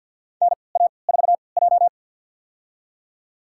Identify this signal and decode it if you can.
Morse code, heard as a keyed tone, about 52 wpm, spaced as for 29 wpm, 700 Hz. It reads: NA4J
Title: USA Call Signs - 3x Character Spacing - 50wpm